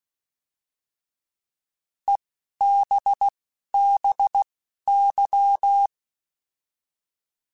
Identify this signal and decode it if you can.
Morse code, heard as a keyed tone, about 16 wpm, spaced as for 11 wpm, 800 Hz. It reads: EBBY
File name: Alphabetical / Alphabetical mp3s / Es @10